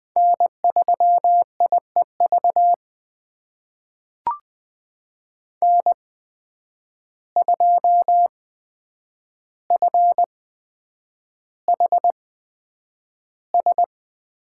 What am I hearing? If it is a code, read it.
Morse code, heard as a keyed tone, about 20 wpm, spaced as for 9 wpm, 700 Hz. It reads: N3IEV N2FHS
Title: USA Call Signs - 5 Characters - 8x Character Spacing - 20wpm